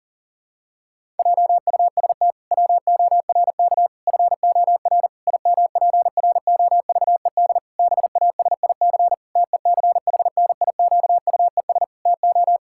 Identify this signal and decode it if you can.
Morse code, heard as a keyed tone, about 40 wpm, 700 Hz. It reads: JUST WORK FOR IMPROVED BASIC TECHNIQUES TO
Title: Pangram Challenge v2 - 40wpm